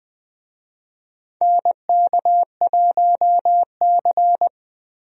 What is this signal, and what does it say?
Morse code, heard as a keyed tone, about 20 wpm, 700 Hz. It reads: NK1C
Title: USA Call Signs - 4 Characters - Repeated 2x - 20wpm